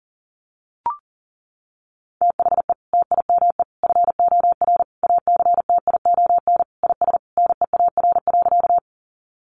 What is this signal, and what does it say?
Morse code, heard as a keyed tone, about 40 wpm, 700 Hz. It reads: THE TIME FOR ACTION IS NEAR.
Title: Sentences from Top 700 Words 40wpm